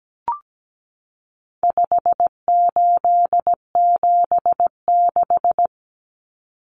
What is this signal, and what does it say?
Morse code, heard as a keyed tone, about 17 wpm, 700 Hz. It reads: 5876
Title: Numbers - 4-Digit 17wpm